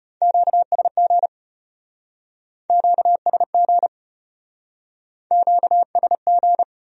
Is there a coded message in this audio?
Yes: Morse code from 0.2 to 6.6 s, heard as a keyed tone, about 36 wpm, 700 Hz. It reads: QSG QSG QSG